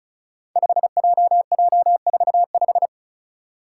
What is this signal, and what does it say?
Morse code, heard as a keyed tone, about 34 wpm, 700 Hz. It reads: 5JJ45
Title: Five Challenging Letter-Number Combinations 35wpm